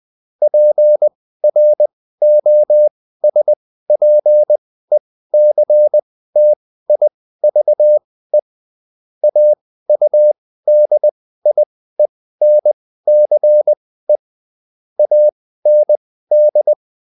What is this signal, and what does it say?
Morse code, heard as a keyed tone, about 20 wpm, spaced as for 14 wpm, 600 Hz. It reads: PROSPECTIVE AUDIENCE AND